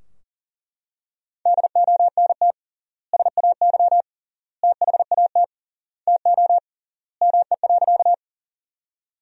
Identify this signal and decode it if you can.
Morse code, heard as a keyed tone, about 40 wpm, 700 Hz. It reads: DONT SAY THAT TO ME.
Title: 3x Word-Spacing - Sentences from Top 300 Words - 40wpm